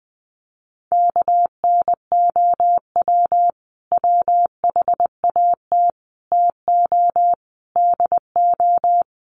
Morse code, heard as a keyed tone, about 20 wpm, 700 Hz. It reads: KNOW WHAT TO DO
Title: Sets of 4 Words 20wpm